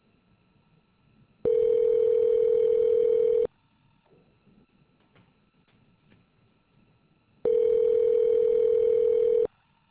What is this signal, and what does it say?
Ringback tone (North America), 440 + 480 Hz.